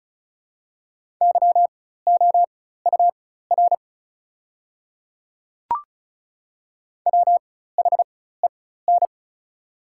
Morse code, heard as a keyed tone, about 34 wpm, spaced as for 16 wpm, 700 Hz. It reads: YOUR WHEN
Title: 4x Character Spacing - Top 200 Words - 35wpm